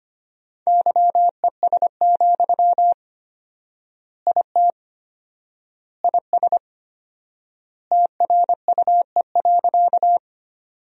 Morse code, heard as a keyed tone, about 25 wpm, 700 Hz. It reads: YES, IT IS TRUE.
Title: 4x Word-Spacing - Sentences from Top 500 Words - 25wpm